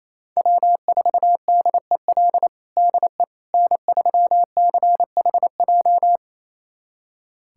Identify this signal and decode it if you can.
Morse code, heard as a keyed tone, about 28 wpm, 700 Hz. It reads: W4DEL DE N3CHJ
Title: USA Callsign DE Callsign - 5 Characters - 28wpm